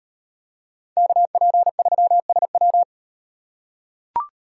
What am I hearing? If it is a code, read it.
Morse code, heard as a keyed tone, about 31 wpm, 700 Hz. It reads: KP3SW